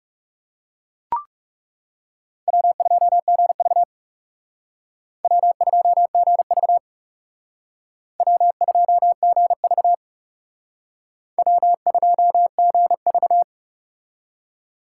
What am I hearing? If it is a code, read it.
Morse code, heard as a keyed tone, about 36 wpm, 700 Hz. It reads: W2GV W2GV W2GV W2GV
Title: Speed Racing - USA Call Signs 30wpm